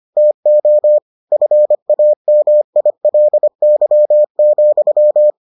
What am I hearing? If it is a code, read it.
Morse code, heard as a keyed tone, about 25 wpm, 600 Hz. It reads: TO FAMILY,